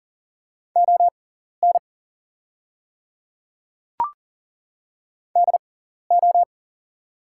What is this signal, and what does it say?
Morse code, heard as a keyed tone, about 40 wpm, spaced as for 14 wpm, 700 Hz. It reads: ON DO
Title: 6x Character Spacing - Top 200 Words - 40wpm